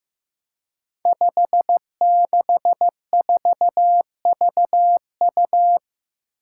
Morse code, heard as a keyed tone, about 15 wpm, 700 Hz. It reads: 564VU
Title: Five Challenging Letter-Number Combinations 15wpm